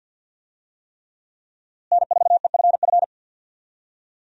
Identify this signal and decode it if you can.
Morse code, heard as a keyed tone, about 50 wpm, 700 Hz. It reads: N4EFF
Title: USA Call Signs - 5 Characters - Repeated 3x - 50wpm